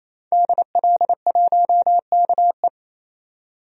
Morse code, heard as a keyed tone, about 28 wpm, 700 Hz. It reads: DL1KE
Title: DXCC-100 Call Signs - 5 Characters - 28wpm